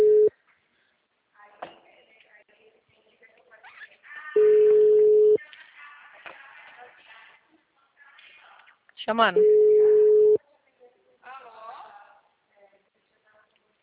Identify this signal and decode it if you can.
Ringback tone (Europe, 425 Hz), with speech.
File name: out- -2039- - - .19386